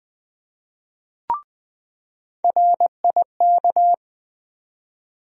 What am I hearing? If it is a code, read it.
Morse code, heard as a keyed tone, about 20 wpm, 700 Hz. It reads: RIK
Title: USA Call Sign Postfixes - 3 Letters 20wpm